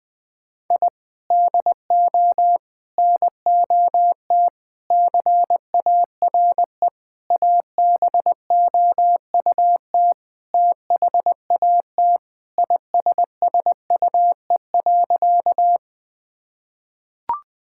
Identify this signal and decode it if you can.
Morse code, heard as a keyed tone, about 20 wpm, 700 Hz. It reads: I DO NOT CARE ABOUT THAT ISSUE.
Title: Sentences from Top 1000 Words - Review Part 6 - 20wpm